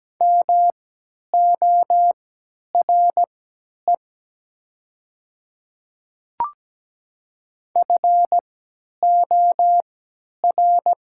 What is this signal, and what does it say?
Morse code, heard as a keyed tone, about 17 wpm, spaced as for 10 wpm, 700 Hz. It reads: MORE FOR
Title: 3x Character Spacing - Top 100 Words - 17wpm